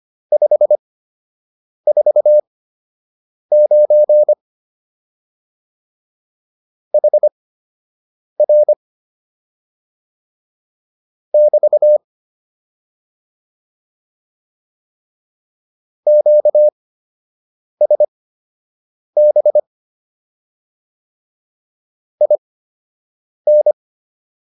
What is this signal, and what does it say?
Morse code, heard as a keyed tone, about 25 wpm, spaced as for 7 wpm, 600 Hz. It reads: 549 HR <BT> QSB IN